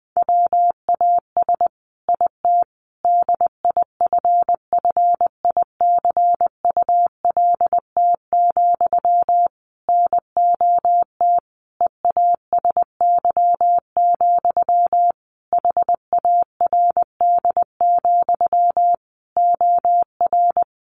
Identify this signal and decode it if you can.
Morse code, heard as a keyed tone, about 20 wpm, 700 Hz. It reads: WAS IT DIFFICULT, NOT EASY, HARD, OR
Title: Sentences from Top 1200 Words 20wpm